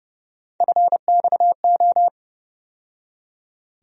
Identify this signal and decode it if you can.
Morse code, heard as a keyed tone, about 31 wpm, 700 Hz. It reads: FXO
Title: USA Call Sign Postfixes - 3 Letters 30wpm